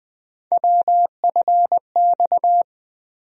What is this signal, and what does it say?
Morse code, heard as a keyed tone, about 20 wpm, 700 Hz. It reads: WFX
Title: USA Call Sign Postfixes - 3 Letters 20wpm